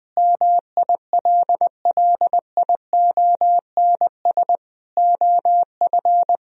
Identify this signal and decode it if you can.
Morse code, heard as a keyed tone, about 20 wpm, 700 Hz. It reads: MILLIONS OF
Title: Speed-Racing - Sets of 5 Words 20wpm